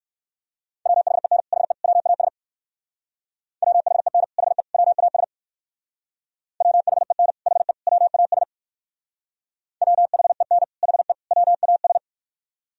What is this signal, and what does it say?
Morse code, heard as a keyed tone, about 66 wpm, 700 Hz. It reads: WHEN HE WAS WHEN HE WAS WHEN HE WAS WHEN HE WAS